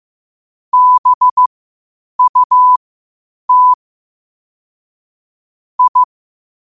Morse code, heard as a keyed tone, about 15 wpm, spaced as for 8 wpm, 1000 Hz. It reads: BUT I
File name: qotdfiles / 8wpm